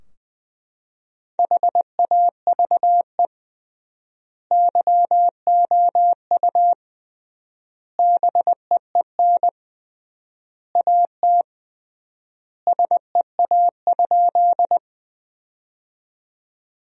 Morse code, heard as a keyed tone, about 20 wpm, 700 Hz. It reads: HAVE YOU BEEN AT SEA?
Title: 3x Word-Spacing - Sentences from Top 300 Words - 20wpm